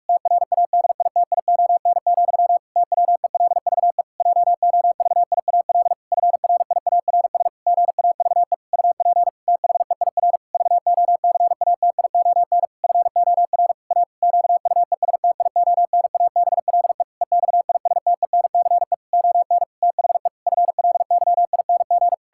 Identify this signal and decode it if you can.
Morse code, heard as a keyed tone, about 41 wpm, 700 Hz. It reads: TRADITION, TWELVE JOVIAL FRIARS GAVE UP THEIR VOCATION FOR A QUESTIONABLE EXISTENCE ON THE FLYING
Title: Pangram Challenge v2 - 45wpm